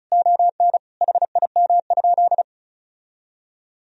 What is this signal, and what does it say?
Morse code, heard as a keyed tone, about 36 wpm, 700 Hz. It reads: ON HIM?
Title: Sentences from Top 1000 Words - Review Part 4 - 35wpm